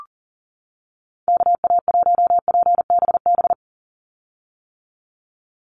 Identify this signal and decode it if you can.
Morse code, heard as a keyed tone, about 38 wpm, 700 Hz. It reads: KA1PBB
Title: USA Call Signs 40wpm